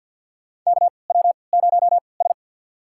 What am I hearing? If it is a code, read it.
Morse code, heard as a keyed tone, about 54 wpm, spaced as for 29 wpm, 700 Hz. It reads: KW0S
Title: USA Call Signs - 4 Characters - 3x Character Spacing - 50wpm